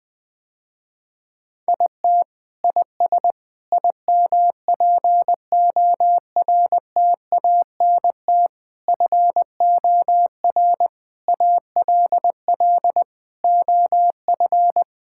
Morse code, heard as a keyed tone, about 21 wpm, 690 Hz. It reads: IT IS IMPORTANT FOR ALL OF
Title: Sentences with 4-Word Phrases - Spoken Beginning - 20wpm